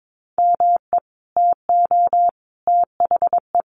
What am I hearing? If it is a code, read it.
Morse code, heard as a keyed tone, about 22 wpm, 700 Hz. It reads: ME TO THE
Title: Sentences from Top 400 Words 22wpm